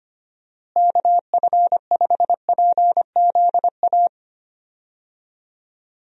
Morse code, heard as a keyed tone, about 25 wpm, 700 Hz. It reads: KF5PZA
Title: USA Call Signs - 6 Characters - Repeated 3x - 25wpm